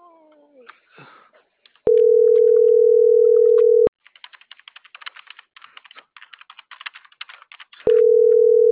Ringback tone (North America), 440 + 480 Hz.